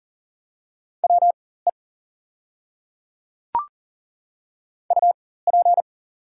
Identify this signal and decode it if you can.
Morse code, heard as a keyed tone, about 40 wpm, spaced as for 19 wpm, 690 Hz. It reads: WE UP
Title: 4x Character Spacing - Top 300 Words - 40wpm